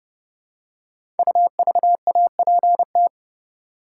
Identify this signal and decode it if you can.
Morse code, heard as a keyed tone, about 28 wpm, 700 Hz. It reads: UVAPT
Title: Code Groups - 5 Letter:Numbers - 30wpm